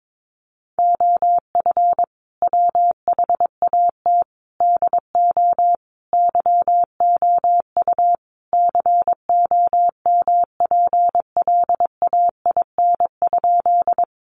Morse code, heard as a keyed tone, about 22 wpm, 700 Hz. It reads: OF WHAT DO YOU COMPLAIN?
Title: Sentences from Top 1300 Words 22wpm